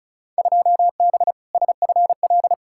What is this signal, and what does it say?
Morse code, heard as a keyed tone, about 36 wpm, 700 Hz. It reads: 2B SFL
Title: Field Day Exchanges Encore - 35wpm